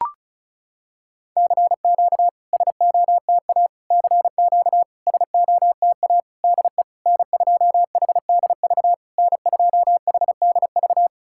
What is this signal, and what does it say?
Morse code, heard as a keyed tone, about 36 wpm, 700 Hz. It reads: CQ SOTA CQ SOTA DE N2HDV N2HDV
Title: SOTA QSO Practice 35wpm